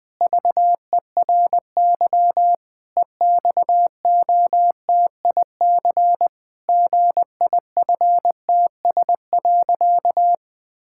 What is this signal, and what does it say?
Morse code, heard as a keyed tone, about 20 wpm, 700 Hz. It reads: VERY EXOTIC GIFTS.
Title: Pangram Challenge v2 - 20wpm